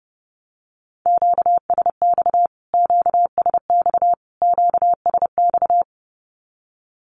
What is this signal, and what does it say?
Morse code, heard as a keyed tone, about 30 wpm, 700 Hz. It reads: QSX QSX QSX
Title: Q-Codes Repeated 3x 30wpm